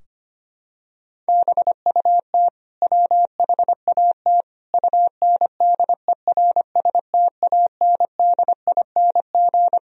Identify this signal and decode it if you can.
Morse code, heard as a keyed tone, about 25 wpm, 700 Hz. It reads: BUT WHAT UNDERSTANDING
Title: Sentences from Top 1000 Words - Review Part 3 - 25wpm